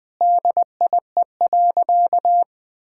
Morse code, heard as a keyed tone, about 20 wpm, 700 Hz. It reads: DIE.
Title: Sentences from Top 1600 Words 20wpm